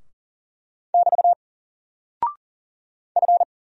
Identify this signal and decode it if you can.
Morse code, heard as a keyed tone, about 39 wpm, 700 Hz. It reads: <BT> F
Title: Single Character (Letter:Number:Punctuation:Prosign) 40wpm